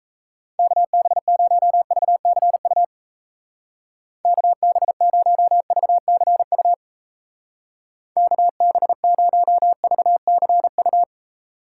Morse code, heard as a keyed tone, about 36 wpm, 700 Hz. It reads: KB0VCU KB0VCU KB0VCU